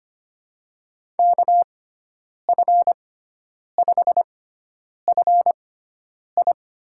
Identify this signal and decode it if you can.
Morse code, heard as a keyed tone, about 25 wpm, spaced as for 9 wpm, 700 Hz. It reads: KF5FI
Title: USA Call Signs - 6 Characters - 6x Character Spacing - 25wpm